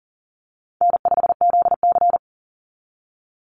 Morse code, heard as a keyed tone, about 40 wpm, 700 Hz. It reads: N5ZC
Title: NA SOTA Chaser Callsigns 40wpm